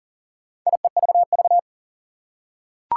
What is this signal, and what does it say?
Morse code, heard as a keyed tone, about 40 wpm, 700 Hz. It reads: IEVV
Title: Four Challenging Letter-Number Combinations 40wpm